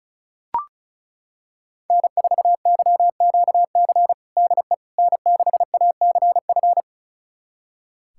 Morse code, heard as a keyed tone, about 36 wpm, 700 Hz. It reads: N4YQC DE N6ACF
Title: USA Callsign DE Callsign - 5 Characters - 35wpm